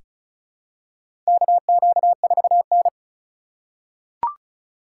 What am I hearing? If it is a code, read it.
Morse code, heard as a keyed tone, about 36 wpm, 700 Hz. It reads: KQ4N
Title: USA Call Signs - 3x Character Spacing - 35wpm